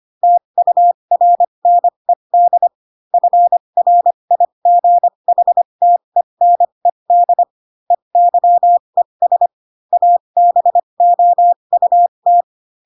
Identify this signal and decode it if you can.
Morse code, heard as a keyed tone, about 25 wpm, spaced as for 22 wpm, 700 Hz. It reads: TURNED FRIGHTENED EYES ABOUT